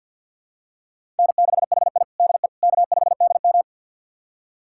Morse code, heard as a keyed tone, about 52 wpm, 700 Hz. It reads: N6HI DE K5DM
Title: USA Callsign DE Callsign - 4 Characters - 50wpm